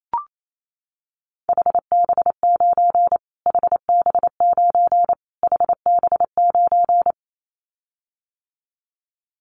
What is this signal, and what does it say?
Morse code, heard as a keyed tone, about 28 wpm, 700 Hz. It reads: HB9 HB9 HB9
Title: Callsign Prefixes for Top 25 DXCC - Repeated 3x - 28wpm